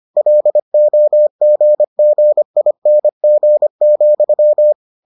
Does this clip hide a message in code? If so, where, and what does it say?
Yes: Morse code from 0.2 to 4.7 s, heard as a keyed tone, about 25 wpm, 600 Hz. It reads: LOGGING,